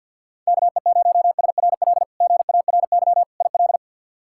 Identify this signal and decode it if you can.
Morse code, heard as a keyed tone, about 51 wpm, 700 Hz. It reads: KE0SRL GARY IL
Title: K1USN SST QSO Practice - 50wpm